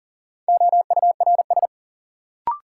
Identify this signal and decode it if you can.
Morse code, heard as a keyed tone, about 40 wpm, 700 Hz. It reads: OURS
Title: Sentences from Top 1300 Words - New Word Drill Alt - 40wpm